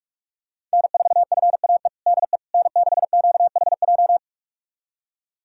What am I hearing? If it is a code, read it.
Morse code, heard as a keyed tone, about 47 wpm, 700 Hz. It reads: N4FAE DE N6QHJ
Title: USA Callsign DE Callsign - 5 Characters - 45wpm